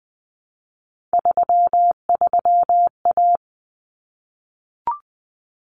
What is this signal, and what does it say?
Morse code, heard as a keyed tone, about 20 wpm, 700 Hz. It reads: 33A
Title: Common Cut Numbers - 3-Digit 20wpm